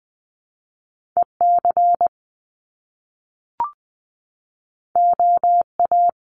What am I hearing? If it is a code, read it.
Morse code, heard as a keyed tone, about 20 wpm, 700 Hz. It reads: E C O A
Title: Top Two Letter Combinations in Words 20wpm